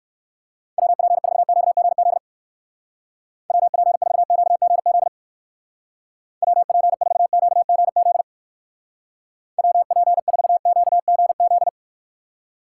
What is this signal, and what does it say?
Morse code, heard as a keyed tone, about 58 wpm, 700 Hz. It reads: WP4QGZ WP4QGZ WP4QGZ WP4QGZ